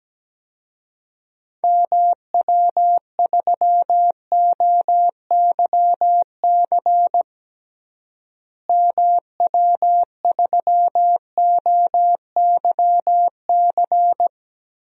Morse code, heard as a keyed tone, about 17 wpm, 700 Hz. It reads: MW3OYC MW3OYC
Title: DXCC-100 Call Signs - 6 Characters - Repeated 2x - 17wpm